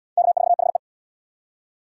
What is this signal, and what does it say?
Morse code, heard as a keyed tone, about 68 wpm, 700 Hz. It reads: K4HE